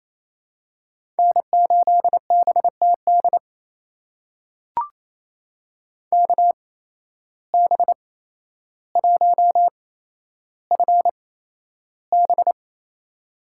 Morse code, heard as a keyed tone, about 28 wpm, spaced as for 12 wpm, 700 Hz. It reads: N8BTD KB1FB
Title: USA Call Signs - 5 Characters - 8x Character Spacing - 28wpm